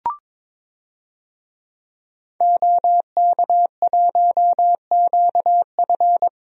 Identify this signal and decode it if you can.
Morse code, heard as a keyed tone, about 22 wpm, 700 Hz. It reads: OK1QF
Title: DXCC-100 Call Signs - 5 Characters - Repeated 2x - 22wpm